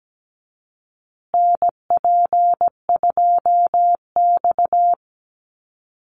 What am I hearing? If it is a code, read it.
Morse code, heard as a keyed tone, about 17 wpm, 700 Hz. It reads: NP2X
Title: Call Signs Repeated 2x - 17wpm